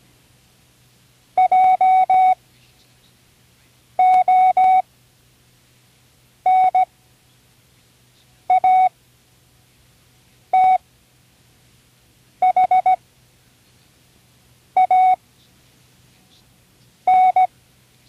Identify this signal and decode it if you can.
Morse code, heard as a keyed tone, about 17 wpm, spaced as for 5 wpm, 740 Hz. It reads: JONATHAN
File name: CWResources / Learning Morse Code (from Tune in the World) / 29 words using all letters